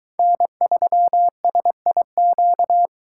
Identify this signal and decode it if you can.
Morse code, heard as a keyed tone, about 22 wpm, 700 Hz. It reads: N3SIQ